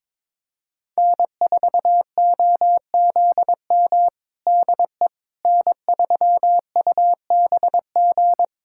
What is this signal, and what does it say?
Morse code, heard as a keyed tone, about 22 wpm, 700 Hz. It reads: N4OZM DE N3UBG
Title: USA Callsign DE Callsign - 5 Characters - 22wpm